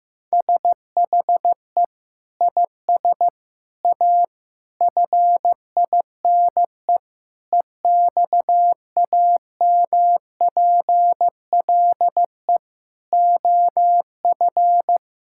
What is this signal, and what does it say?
Morse code, heard as a keyed tone, about 16 wpm, 700 Hz. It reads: SHE IS A FINE EXAMPLE OF
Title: Sentences from Top 1000 Words - Review Part 6 - 15wpm